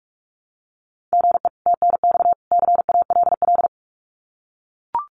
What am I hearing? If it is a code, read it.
Morse code, heard as a keyed tone, about 44 wpm, 700 Hz. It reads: GE TNX CALL
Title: Common QSO Phrases 45wpm